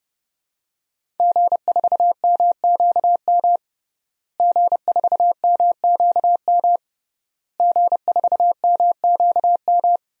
Morse code, heard as a keyed tone, about 32 wpm, 700 Hz. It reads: G4MQM G4MQM G4MQM
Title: DXCC-100 Call Signs - 5 Characters - Repeated 3x - 30wpm